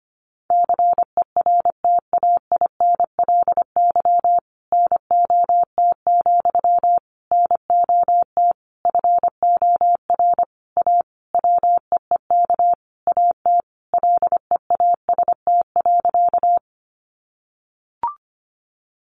Morse code, heard as a keyed tone, about 25 wpm, 700 Hz. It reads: CERTAINLY NOT, NOT FOR A WEEK AT LEAST.
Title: Sentences from Top 900 Words 25wpm